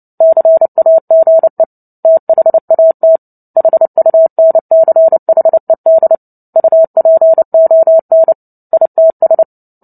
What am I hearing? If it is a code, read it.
Morse code, heard as a keyed tone, about 30 wpm, 650 Hz. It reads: CAGE THAT HUNCHED UPON ITS
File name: Book Two / 03-Chapter 3 - 29 WPM